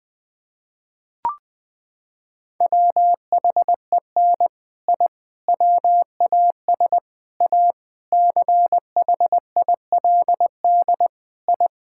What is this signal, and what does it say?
Morse code, heard as a keyed tone, about 20 wpm, 700 Hz. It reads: WHEN I WAS A CHILD I
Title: Sentences with 3-Word Phrases - 20wpm